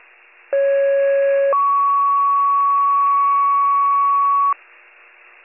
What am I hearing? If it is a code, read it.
Two-tone page, heard as a one-second tone, then a three-second tone, about 550 then 1120 Hz. It is heard from 0.5 to 4.5 s.